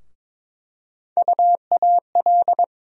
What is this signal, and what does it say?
Morse code, heard as a keyed tone, about 22 wpm, 700 Hz. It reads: UAL
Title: USA Call Sign Postfixes - 3 Letters 22wpm